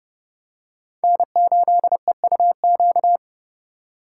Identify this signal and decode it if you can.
Morse code, heard as a keyed tone, about 30 wpm, 700 Hz. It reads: N8EUQ
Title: USA Call Signs - 5 Characters - 2x Character Spacing - 30wpm